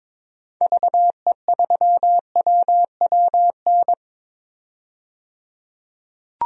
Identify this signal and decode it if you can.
Morse code, heard as a keyed tone, about 22 wpm, 700 Hz. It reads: VE3WWN
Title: DXCC-100 Call Signs - 6 Characters - 22wpm